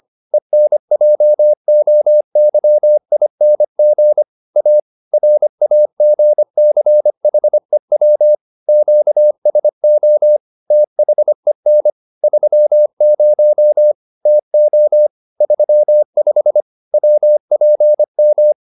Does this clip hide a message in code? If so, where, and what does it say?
Yes: Morse code from 0.3 to 18.5 s, heard as a keyed tone, about 25 wpm, 600 Hz. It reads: ENJOYING A RAGCHEW QSO THEN 30 TO 35 WPM